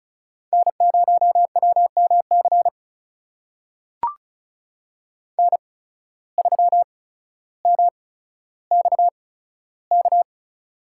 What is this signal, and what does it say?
Morse code, heard as a keyed tone, about 36 wpm, spaced as for 15 wpm, 700 Hz. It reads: N0WMC N3MXK
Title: USA Call Signs - 5 Characters - 8x Character Spacing - 35wpm